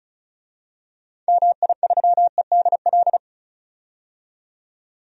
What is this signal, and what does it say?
Morse code, heard as a keyed tone, about 34 wpm, 700 Hz. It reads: MI3EDL
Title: DXCC-100 Call Signs - 6 Characters - 35wpm